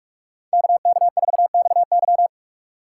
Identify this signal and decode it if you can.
Morse code, heard as a keyed tone, about 46 wpm, 700 Hz. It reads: KK4XY
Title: USA Call Signs - 2x Character Spacing - 45wpm